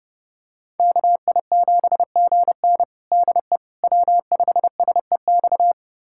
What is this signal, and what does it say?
Morse code, heard as a keyed tone, about 30 wpm, 700 Hz. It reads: KI7GN DE W5SEX
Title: USA Callsign DE Callsign - 5 Characters - 30wpm